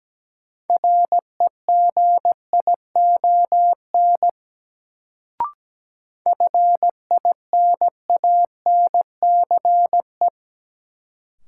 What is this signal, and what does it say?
Morse code, heard as a keyed tone, about 17 wpm, 700 Hz. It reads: REGION FINANCE
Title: Sentences from Top 1500 Words - New Word Drill - 17wpm